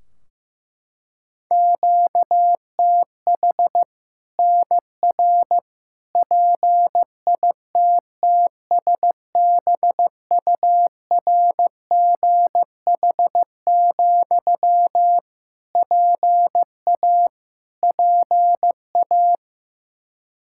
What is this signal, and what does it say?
Morse code, heard as a keyed tone, about 15 wpm, 700 Hz. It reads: QTH NR PITTSBURGH, PA PA
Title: Practice QSO - Exchange 1 - Beginning - 15wpm